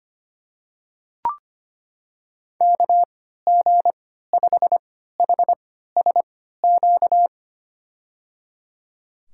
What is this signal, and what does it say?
Morse code, heard as a keyed tone, about 25 wpm, spaced as for 14 wpm, 700 Hz. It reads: KG5HSQ
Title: USA Call Signs - 6 Characters - 3x Character Spacing - 25wpm